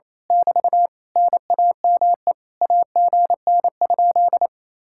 Morse code, heard as a keyed tone, about 27 wpm, 700 Hz. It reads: <BT> NAME AGN?
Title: Speed-Racing Common QSO Phrases - 28wpm